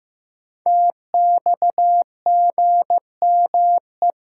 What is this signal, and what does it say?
Morse code, heard as a keyed tone, about 15 wpm, 700 Hz. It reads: TXGME
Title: Code Groups - 5 Letter:Numbers - 15wpm